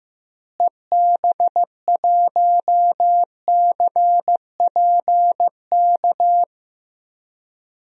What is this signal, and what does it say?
Morse code, heard as a keyed tone, about 16 wpm, 700 Hz. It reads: EB1CPK
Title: DXCC-100 Call Signs - 6 Characters - Repeated 3x - 15wpm